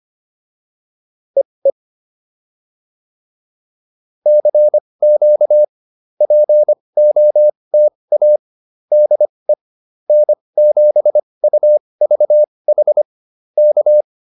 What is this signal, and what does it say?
Morse code, heard as a keyed tone, about 25 wpm, spaced as for 20 wpm, 600 Hz. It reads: EE CQ POTA DE N7UVH K